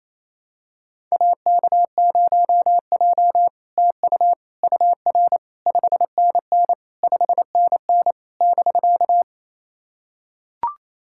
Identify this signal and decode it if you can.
Morse code, heard as a keyed tone, about 28 wpm, 700 Hz. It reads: AK0J TU UR 5NN 5NN <BK>
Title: SOTA QSO Practice 28wpm